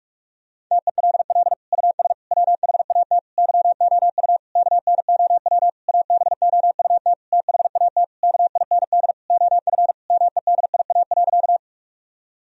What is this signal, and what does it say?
Morse code, heard as a keyed tone, about 46 wpm, 700 Hz. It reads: TELL US WHAT YOU KNOW ABOUT THAT KIND OF MEDIA.
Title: Sentences from Top 1000 Words - Review Part 4 - 45wpm